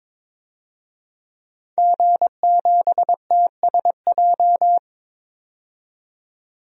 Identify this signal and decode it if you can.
Morse code, heard as a keyed tone, about 22 wpm, 700 Hz. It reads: G7TSJ
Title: DXCC-100 Call Signs - 5 Characters - 22wpm